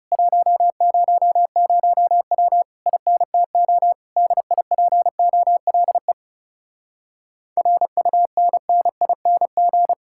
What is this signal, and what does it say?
Morse code, heard as a keyed tone, about 40 wpm, 700 Hz. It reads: 100W INTO DIPOLE RUNNING